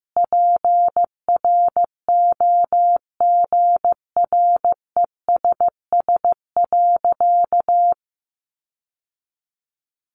Morse code, heard as a keyed tone, about 15 wpm, 700 Hz. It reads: PROGRESS.
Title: Sentences from Top 1400 Words 15wpm